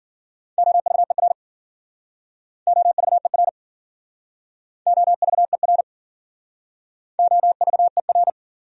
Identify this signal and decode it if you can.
Morse code, heard as a keyed tone, about 50 wpm, 700 Hz. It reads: OVER OVER OVER OVER
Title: Speed-Racing Top 100 Words 40wpm